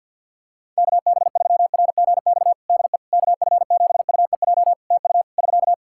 Morse code, heard as a keyed tone, about 50 wpm, 700 Hz. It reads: KB3RGX DE KF7UEJ TU <SK>
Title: Practice QSO - Exchange 4 - Ending - 50wpm